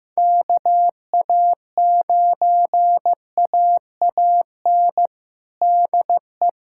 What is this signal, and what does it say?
Morse code, heard as a keyed tone, about 15 wpm, 700 Hz. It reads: KA9AAN DE
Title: USA Callsign DE Callsign 15wpm